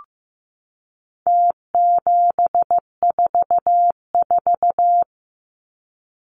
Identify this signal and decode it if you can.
Morse code, heard as a keyed tone, about 15 wpm, 700 Hz. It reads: T744
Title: Numbers - Contest Style - 4-Digit 15wpm